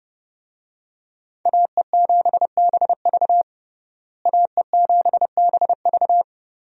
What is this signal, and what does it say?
Morse code, heard as a keyed tone, about 30 wpm, 700 Hz. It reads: AE7BV AE7BV
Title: USA Call Signs - 5 Characters - Repeated 3x - 30wpm